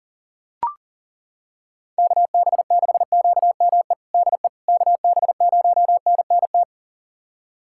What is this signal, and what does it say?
Morse code, heard as a keyed tone, about 40 wpm, 700 Hz. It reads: KB6QME DE KB0NNT
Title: USA Callsign DE Callsign - 6 Characters - 40wpm